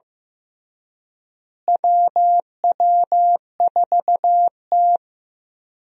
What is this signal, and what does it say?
Morse code, heard as a keyed tone, about 15 wpm, 700 Hz. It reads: WW4T
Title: USA Call Signs - 4 Characters - Repeated 3x - 15wpm